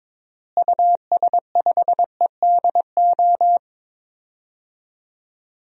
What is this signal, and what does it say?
Morse code, heard as a keyed tone, about 23 wpm, 700 Hz. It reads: US5EDO
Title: DXCC-100 Call Signs - 6 Characters - Repeated 2x - 22wpm